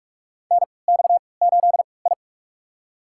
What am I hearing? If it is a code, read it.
Morse code, heard as a keyed tone, about 44 wpm, spaced as for 25 wpm, 690 Hz. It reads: NX8I
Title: USA Call Signs - 4 Characters - 3x Character Spacing - 45wpm